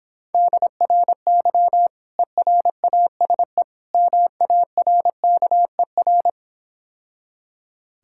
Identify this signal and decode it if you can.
Morse code, heard as a keyed tone, about 26 wpm, 700 Hz. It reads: DRY ERASE MARKER